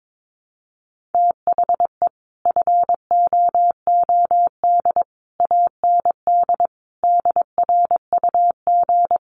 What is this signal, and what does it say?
Morse code, heard as a keyed tone, about 22 wpm, 700 Hz. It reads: THE FOOD AND DRUG
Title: Sets of 5 Words 22wpm